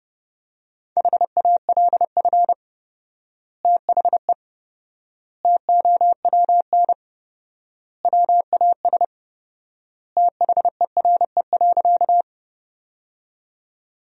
Morse code, heard as a keyed tone, about 30 wpm, 700 Hz. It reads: HALF THE TOWN WAS THERE.
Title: 4x Word-Spacing - Sentences from Top 500 Words - 30wpm